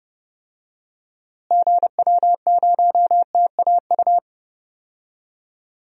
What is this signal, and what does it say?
Morse code, heard as a keyed tone, about 31 wpm, 700 Hz. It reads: GW0TAU
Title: DXCC-100 Call Signs - 6 Characters - Repeated 3x - 30wpm